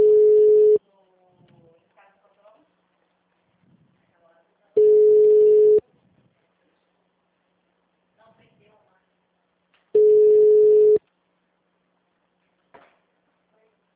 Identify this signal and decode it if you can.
Ringback tone (Europe, 425 Hz), with speech.
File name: out- -2015- - - .17502